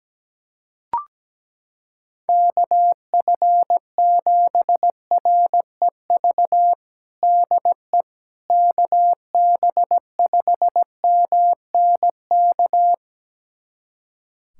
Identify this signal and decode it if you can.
Morse code, heard as a keyed tone, about 17 wpm, 700 Hz. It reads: KF7REV DE KB5MNK
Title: USA Callsign DE Callsign - 6 Characters - 17wpm